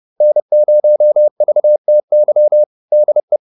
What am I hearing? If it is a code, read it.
Morse code, heard as a keyed tone, about 30 wpm, 600 Hz. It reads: N0VTY DE